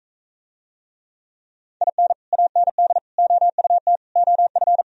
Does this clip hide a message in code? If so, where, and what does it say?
Yes: Morse code from 1.8 to 4.8 s, heard as a keyed tone, about 42 wpm, 700 Hz. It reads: IN AND OUT OF